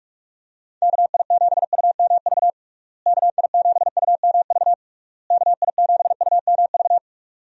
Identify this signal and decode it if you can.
Morse code, heard as a keyed tone, about 41 wpm, 700 Hz. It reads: KI7UMV KI7UMV KI7UMV
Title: USA Call Signs - 6 Characters - Repeated 3x - 45wpm